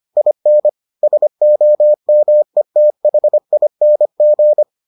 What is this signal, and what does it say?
Morse code, heard as a keyed tone, about 25 wpm, 600 Hz. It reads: IN SOMETHING